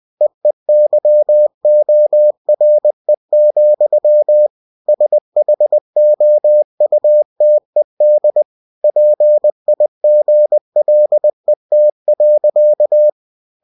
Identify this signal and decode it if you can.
Morse code, heard as a keyed tone, about 20 wpm, 600 Hz. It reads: EEYORE, SHOUTED PIGLET.
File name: cw-books / winnie-the-pooh / winnie-20wpm-page-0053